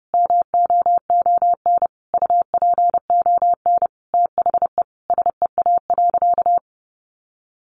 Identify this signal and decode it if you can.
Morse code, heard as a keyed tone, about 30 wpm, 700 Hz. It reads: MOON UPON THE SEA.
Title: Sentences from Top 1500 Words 30wpm